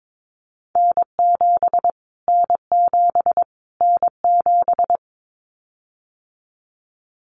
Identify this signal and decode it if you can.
Morse code, heard as a keyed tone, about 22 wpm, 700 Hz. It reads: N7 N7 N7
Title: Callsign Prefixes for Top 25 DXCC - Repeated 3x - 22wpm